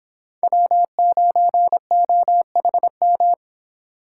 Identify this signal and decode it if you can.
Morse code, heard as a keyed tone, about 29 wpm, 700 Hz. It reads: W9OHM